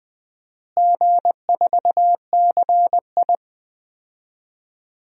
Morse code, heard as a keyed tone, about 20 wpm, 700 Hz. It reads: G4CI
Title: DXCC-100 Call Signs - 4 Characters - Repeated 2x - 20wpm